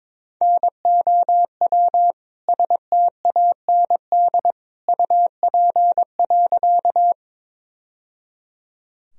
Morse code, heard as a keyed tone, about 22 wpm, 700 Hz. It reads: NOW STAND UP.
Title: Sentences from Top 1000 Words - Review Part 1 - 22wpm